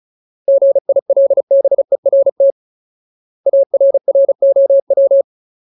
Morse code, heard as a keyed tone, about 34 wpm, 550 Hz. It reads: GILBERT ARROW
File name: Ultimate Word List / 35WPM Ultimate